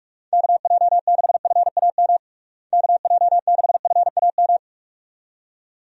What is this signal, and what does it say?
Morse code, heard as a keyed tone, about 46 wpm, 700 Hz. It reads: KJ6FAM KJ6FAM
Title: USA Call Signs - 6 Characters - Repeated 2x - 45wpm